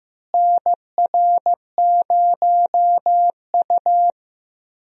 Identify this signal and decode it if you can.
Morse code, heard as a keyed tone, about 16 wpm, 700 Hz. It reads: NR0U
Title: USA Call Signs - 4 Characters - Repeated 3x - 15wpm